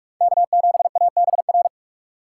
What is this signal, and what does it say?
Morse code, heard as a keyed tone, about 47 wpm, 700 Hz. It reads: K7ABR
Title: USA Call Signs - 5 Characters - 4x Character Spacing - 45wpm